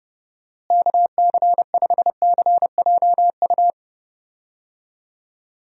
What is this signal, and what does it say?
Morse code, heard as a keyed tone, about 30 wpm, 700 Hz. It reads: KC5CJU
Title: USA Call Signs - 6 Characters - Repeated 3x - 30wpm